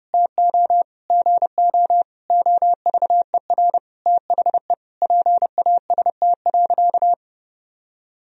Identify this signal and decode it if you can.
Morse code, heard as a keyed tone, about 30 wpm, 700 Hz. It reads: TO GO OVER THE PAST.
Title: Sentences from Top 1000 Words - Review Part 3 - 30wpm